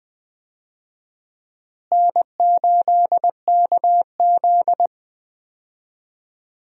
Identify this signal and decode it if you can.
Morse code, heard as a keyed tone, about 20 wpm, 700 Hz. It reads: N8KZ
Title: USA Call Signs - 4 Characters - Repeated 2x - 20wpm